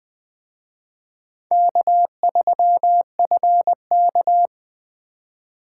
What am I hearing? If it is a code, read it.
Morse code, heard as a keyed tone, about 25 wpm, 700 Hz. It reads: K3FK